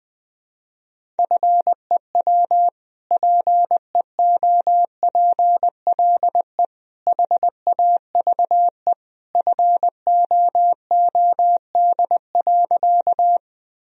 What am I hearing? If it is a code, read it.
Morse code, heard as a keyed tone, about 20 wpm, 700 Hz. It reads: FEW PEOPLE HAVE FOOD.
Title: Sentences from Top 1000 Words - Review Part 1 - 20wpm